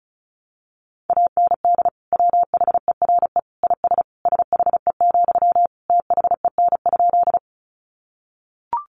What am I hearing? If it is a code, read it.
Morse code, heard as a keyed tone, about 34 wpm, 700 Hz. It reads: AND WHERE IS SHE, THEN?
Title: Sentences from Top 200 Words 35wpm